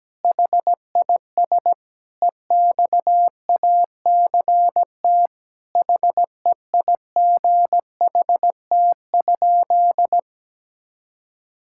Morse code, heard as a keyed tone, about 17 wpm, 700 Hz. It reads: HIS EXACT HEIGHT?
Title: Sentences from Top 1600 Words 17wpm